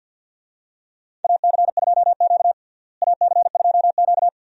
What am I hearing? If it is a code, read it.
Morse code, heard as a keyed tone, about 47 wpm, 700 Hz. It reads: AC2Q AC2Q
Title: USA Call Signs - 4 Characters - Repeated 2x - 50wpm